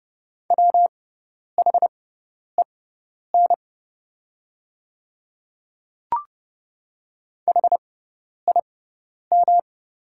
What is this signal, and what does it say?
Morse code, heard as a keyed tone, about 31 wpm, spaced as for 10 wpm, 700 Hz. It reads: WHEN HIM
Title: 6x Character Spacing - Top 100 Words - 30wpm